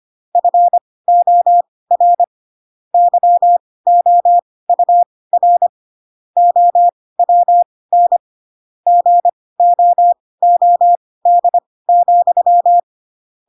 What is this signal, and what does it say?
Morse code, heard as a keyed tone, about 25 wpm, spaced as for 18 wpm, 700 Hz. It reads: FOR YOUR OWN GOOD,
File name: Pinocchio / Pinocchio ch4 12 25 ik1hge